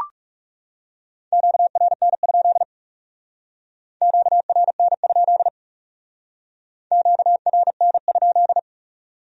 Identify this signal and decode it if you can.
Morse code, heard as a keyed tone, about 38 wpm, 700 Hz. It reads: QRN? QRN? QRN?